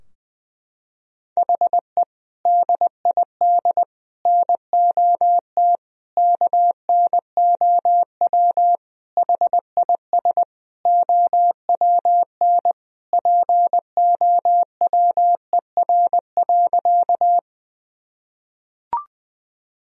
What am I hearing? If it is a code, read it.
Morse code, heard as a keyed tone, about 20 wpm, 700 Hz. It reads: HE DID NOT KNOW HIS OWN POWER.
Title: Sentences from Top 1000 Words - Review Part 2 - 20wpm